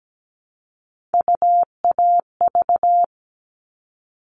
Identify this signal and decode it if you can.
Morse code, heard as a keyed tone, about 17 wpm, 700 Hz. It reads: UAV
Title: All Cut Numbers - 3-Digit 17wpm